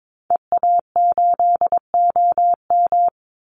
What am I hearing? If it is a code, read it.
Morse code, heard as a keyed tone, about 22 wpm, 700 Hz. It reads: EA8OM
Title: Call Signs Repeated 3x - 22wpm